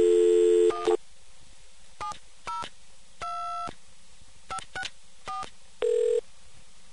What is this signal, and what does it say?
DTMF, heard as telephone touch tones, tones of about 150 ms. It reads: 1 702564